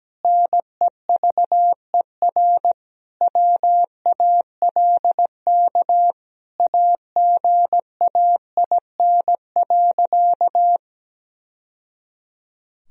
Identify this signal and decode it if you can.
Morse code, heard as a keyed tone, about 17 wpm, 700 Hz. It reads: NEVER WALK AGAIN.
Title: Sentences from Top 1000 Words - Review Part 1 - 17wpm